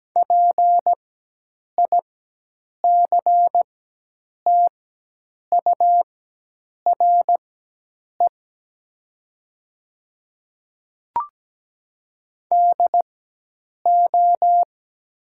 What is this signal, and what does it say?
Morse code, heard as a keyed tone, about 17 wpm, spaced as for 8 wpm, 700 Hz. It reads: PICTURE DO
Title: 4x Character Spacing - Top 200 Words - 17wpm